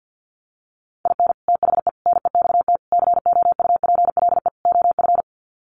Morse code, heard as a keyed tone, about 50 wpm, 700 Hz. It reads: IN THE NEXT COUPLE OF
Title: Sets of 5 Words Encore2 50wpm